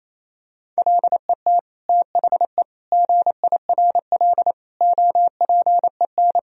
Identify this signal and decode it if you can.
Morse code, heard as a keyed tone, about 28 wpm, 700 Hz. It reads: LET THE GIRL OPEN
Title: Sentences from Top 1000 Words - Review Part 1 - 28wpm